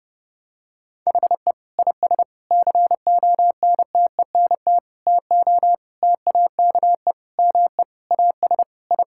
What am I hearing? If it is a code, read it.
Morse code, heard as a keyed tone, about 32 wpm, 700 Hz. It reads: HE IS CONTENT TO TAKE ME AS I
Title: Sentences from Top 1000 Words - Review Part 6 - 30wpm